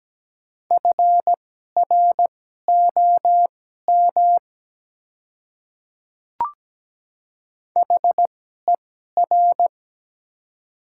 Morse code, heard as a keyed tone, about 17 wpm, spaced as for 12 wpm, 700 Hz. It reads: FROM HER
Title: 2x Character Spacing - Top 100 Words - 17wpm